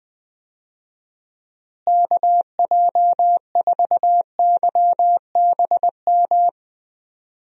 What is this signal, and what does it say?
Morse code, heard as a keyed tone, about 21 wpm, 700 Hz. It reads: KJ4YBM
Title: USA Call Signs - 6 Characters - Repeated 2x - 20wpm